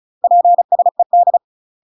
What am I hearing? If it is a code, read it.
Morse code, heard as a keyed tone, about 35 wpm, 700 Hz. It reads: PSED
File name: UK MASTER QSOs TRAINING / 35WPM / UK MASTER QSOs 0026